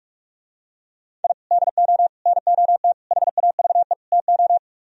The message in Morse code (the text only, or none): I DO NOT HAVE TO